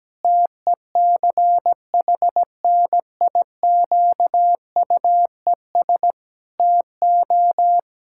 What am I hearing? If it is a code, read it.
Morse code, heard as a keyed tone, about 17 wpm, 700 Hz. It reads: TECHNIQUES TO
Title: Pangram Challenge v2 - 17wpm